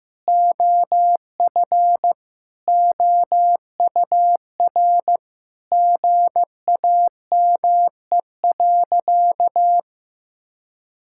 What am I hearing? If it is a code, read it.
Morse code, heard as a keyed tone, about 16 wpm, 700 Hz. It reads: OF OUR GAME.
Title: Sentences from Top 1000 Words - Review Part 5 - 15wpm